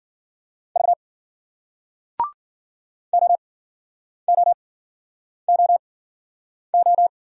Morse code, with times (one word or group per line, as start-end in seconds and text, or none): 0.8-0.9 V
3.1-7.1 OOOO